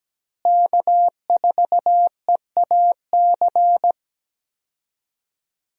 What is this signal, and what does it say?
Morse code, heard as a keyed tone, about 17 wpm, 700 Hz. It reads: K4EAC
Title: USA Call Signs - 5 Characters - Repeated 3x - 17wpm